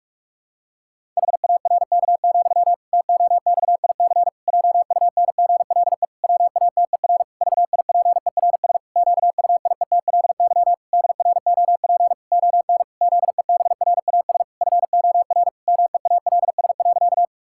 Morse code, heard as a keyed tone, about 46 wpm, 700 Hz. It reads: HARK, TOXIC JUNGLE WATER VIPERS QUIETLY DROP ON ZEBRAS FOR MEALS.
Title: Pangram Challenge v2 - 45wpm